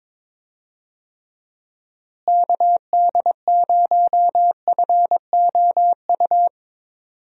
Morse code, heard as a keyed tone, about 22 wpm, 700 Hz. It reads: KD0FOU
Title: USA Call Signs - 6 Characters - Repeated 2x - 22wpm